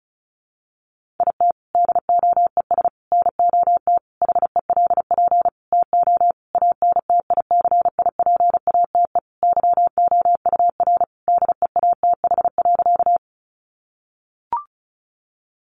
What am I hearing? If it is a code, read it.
Morse code, heard as a keyed tone, about 34 wpm, 700 Hz. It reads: IT DOES NOT HELP TO ANTICIPATE YOUR DEATH.
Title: Sentences from Top 1500 Words 35wpm